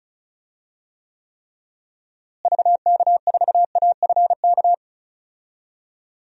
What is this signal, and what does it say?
Morse code, heard as a keyed tone, about 36 wpm, 700 Hz. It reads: VK4AFK
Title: DXCC-100 Call Signs - 6 Characters - Repeated 3x - 35wpm